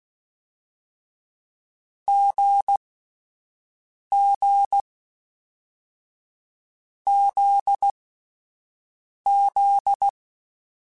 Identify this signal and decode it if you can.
Morse code, heard as a keyed tone, about 16 wpm, 800 Hz. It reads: G G Z Z
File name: Short Runs / Short Runs-mp3s / awj1ndb6gz7 @10